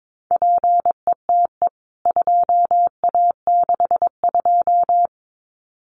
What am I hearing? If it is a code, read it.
Morse code, heard as a keyed tone, about 22 wpm, 700 Hz. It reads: PETE 2A62
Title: CWT Exchange 22wpm